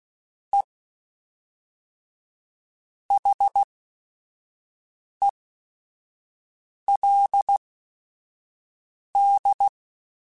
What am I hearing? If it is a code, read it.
Morse code, heard as a keyed tone, about 16 wpm, spaced as for 5 wpm, 800 Hz. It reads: E HELD